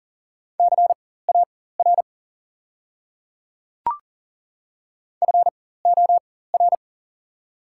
Morse code, heard as a keyed tone, about 38 wpm, spaced as for 18 wpm, 690 Hz. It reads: CAR FOR
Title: 4x Character Spacing - Top 300 Words - 40wpm